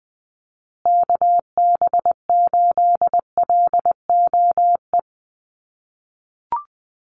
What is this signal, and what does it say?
Morse code, heard as a keyed tone, about 20 wpm, 700 Hz. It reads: KB8LOE
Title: USA Call Signs Repeated 2x 20wpm